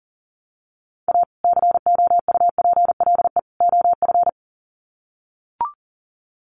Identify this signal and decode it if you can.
Morse code, heard as a keyed tone, about 40 wpm, 700 Hz. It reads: A COUPLE OF
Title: Sets of 3 Words 40wpm